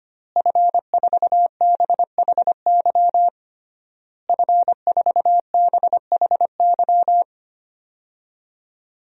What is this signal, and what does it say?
Morse code, heard as a keyed tone, about 25 wpm, 700 Hz. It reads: F4BHY F4BHY
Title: DXCC-100 Call Signs - 5 Characters - Repeated 3x - 25wpm